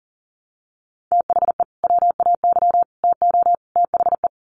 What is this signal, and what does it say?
Morse code, heard as a keyed tone, about 40 wpm, 700 Hz. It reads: THE WAY TO THE
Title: Sets of 4 Words 40wpm